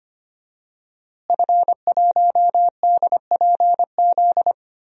Morse code, heard as a keyed tone, about 25 wpm, 700 Hz. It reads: F1DPZ
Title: DXCC-100 Call Signs - 5 Characters - Repeated 3x - 25wpm